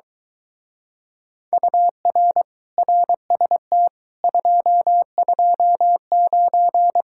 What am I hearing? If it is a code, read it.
Morse code, heard as a keyed tone, about 23 wpm, 700 Hz. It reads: UR RST 229